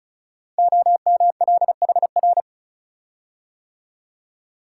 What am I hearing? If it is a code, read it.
Morse code, heard as a keyed tone, about 36 wpm, 700 Hz. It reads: OMLHR
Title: Code Groups - 5 Letters - 35wpm